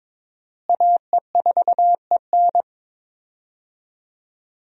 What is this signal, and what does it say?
Morse code, heard as a keyed tone, about 22 wpm, 700 Hz. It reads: AE4EN
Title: USA Call Signs - 5 Characters - Repeated 3x - 22wpm